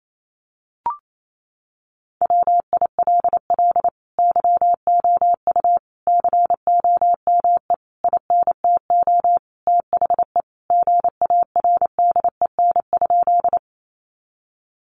Morse code, heard as a keyed tone, about 28 wpm, 700 Hz. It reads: WILL YOU COME INTO THE GARDEN?
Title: Sentences from Top 700 Words 28wpm